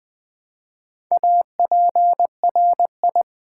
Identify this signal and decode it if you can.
Morse code, heard as a keyed tone, about 20 wpm, 700 Hz. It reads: APRI
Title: Build A Word - 1 Letter At A Time - 5-Letter Words - 20wpm